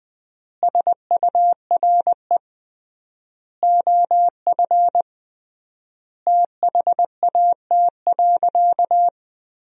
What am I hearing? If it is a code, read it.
Morse code, heard as a keyed tone, about 21 wpm, 700 Hz. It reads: SURE OF THAT.
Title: 3x Word-Spacing - Sentences from Top 300 Words - 20wpm